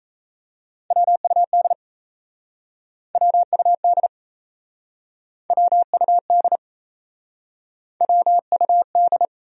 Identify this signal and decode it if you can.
Morse code, heard as a keyed tone, about 39 wpm, 700 Hz. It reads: WUD WUD WUD WUD